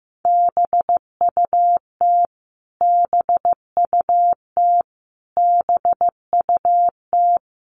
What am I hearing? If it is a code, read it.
Morse code, heard as a keyed tone, about 15 wpm, 700 Hz. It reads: BUT BUT BUT
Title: Top 200 Words Repeated 15wpm